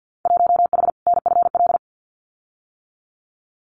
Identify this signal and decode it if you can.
Morse code, heard as a keyed tone, about 50 wpm, 700 Hz. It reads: 1H NFL
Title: Winter Field Day Exchanges 50wpm